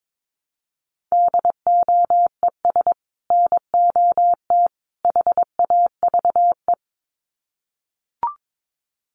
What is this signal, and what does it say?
Morse code, heard as a keyed tone, about 22 wpm, 700 Hz. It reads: DOES NOT HAVE
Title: Sets of 3 Words 22wpm